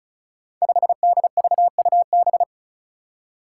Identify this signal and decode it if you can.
Morse code, heard as a keyed tone, about 36 wpm, 700 Hz. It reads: 5DVUB
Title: Five Challenging Letter-Number Combinations 35wpm